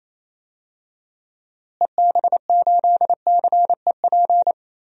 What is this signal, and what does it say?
Morse code, heard as a keyed tone, about 28 wpm, 700 Hz. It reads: EB8CEP
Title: DXCC-100 Call Signs - 6 Characters - Repeated 3x - 28wpm